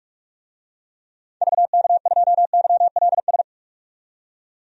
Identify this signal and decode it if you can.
Morse code, heard as a keyed tone, about 46 wpm, 700 Hz. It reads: VK2YLS
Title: DXCC-100 Call Signs - 6 Characters - 45wpm